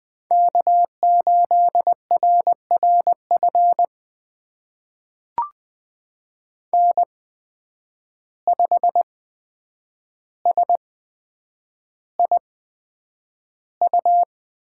Morse code, heard as a keyed tone, about 20 wpm, spaced as for 9 wpm, 700 Hz. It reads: K8RRF N5SIU
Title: USA Call Signs - 8x Character Spacing - 20wpm